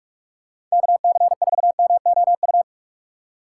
Morse code, heard as a keyed tone, about 42 wpm, 700 Hz. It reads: KC4MOU
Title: USA Call Signs - 6 Characters - 3x Character Spacing - 45wpm